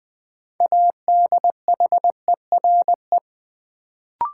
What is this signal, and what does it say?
Morse code, heard as a keyed tone, about 20 wpm, 700 Hz. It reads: ADHERE
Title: Build A Word - 3 Letters At A Time - 6-Letter Words - 20wpm